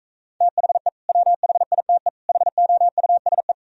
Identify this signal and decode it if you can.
Morse code, heard as a keyed tone, about 45 wpm, 700 Hz. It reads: THE WHITE HOUSE